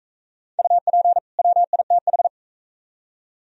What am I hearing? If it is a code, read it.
Morse code, heard as a keyed tone, about 44 wpm, 700 Hz. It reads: UP WITH